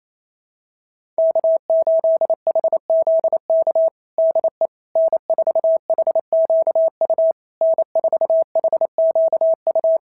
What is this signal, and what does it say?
Morse code, heard as a keyed tone, about 28 wpm, 660 Hz. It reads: K8HZK DE N4HQU N4HQU
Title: SOTA QSO Practice 28wpm